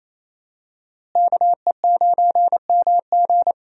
Morse code, heard as a keyed tone, about 27 wpm, 700 Hz. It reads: KE9MG
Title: USA Call Signs - 5 Characters - Repeated 3x - 28wpm